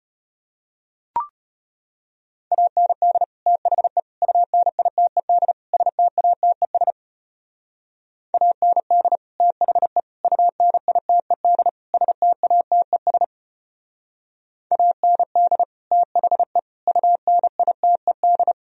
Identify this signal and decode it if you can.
Morse code, heard as a keyed tone, about 30 wpm, 700 Hz. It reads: AND THE UNITED STATES AND THE UNITED STATES AND THE UNITED
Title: Speed-Racing - Sets of 4 Words 25wpm